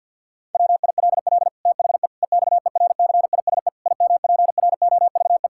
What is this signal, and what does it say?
Morse code, heard as a keyed tone, about 52 wpm, 700 Hz. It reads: WILL THE EXERCISE IMPROVE
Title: Sentences from Top 1000 Words - Review Part 6 - 50wpm